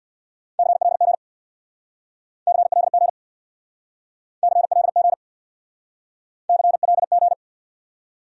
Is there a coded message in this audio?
Yes: Morse code from 0.6 to 7.3 s, heard as a keyed tone, about 69 wpm, 700 Hz. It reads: CLG CLG CLG CLG